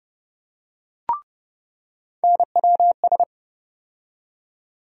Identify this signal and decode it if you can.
Morse code, heard as a keyed tone, about 31 wpm, 700 Hz. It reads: NWS
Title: USA Call Sign Postfixes - 3 Letters 30wpm